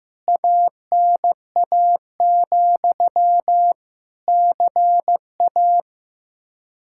Morse code, heard as a keyed tone, about 15 wpm, 700 Hz. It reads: ANA, CA
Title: K1USN SST QSO Practice 15wpm